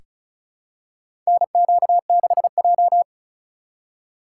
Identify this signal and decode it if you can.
Morse code, heard as a keyed tone, about 36 wpm, 700 Hz. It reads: NQ6J
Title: USA Call Signs - 4 Characters - 4x Character Spacing - 35wpm